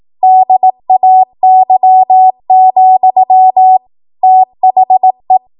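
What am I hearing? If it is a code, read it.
Morse code, heard as a keyed tone, about 18 wpm, 750 Hz. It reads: DAY, THE
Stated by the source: 18 WPM Morse code practice from a national amateur radio association's code archive.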